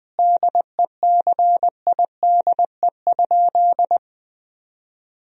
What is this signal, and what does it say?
Morse code, heard as a keyed tone, about 20 wpm, 700 Hz. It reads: DECIDE?
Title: Sentences from Top 1000 Words - Review Part 3 - 20wpm